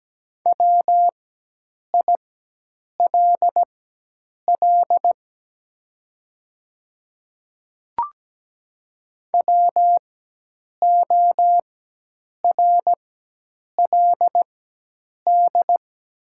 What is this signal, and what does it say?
Morse code, heard as a keyed tone, about 17 wpm, spaced as for 8 wpm, 700 Hz. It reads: WILL WORLD
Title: 4x Character Spacing - Top 300 Words - 17wpm